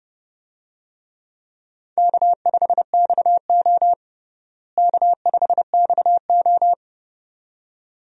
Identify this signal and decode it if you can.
Morse code, heard as a keyed tone, about 32 wpm, 700 Hz. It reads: K5XO K5XO
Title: USA Call Signs - 4 Characters - Repeated 2x - 30wpm